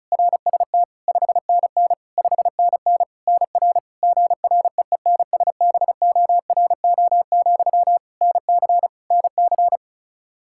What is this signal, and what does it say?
Morse code, heard as a keyed tone, about 36 wpm, 700 Hz. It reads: RST 5NN 5NN NR GREENSBORO, NC NC
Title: Practice QSO - Exchange 1 - Beginning - 35wpm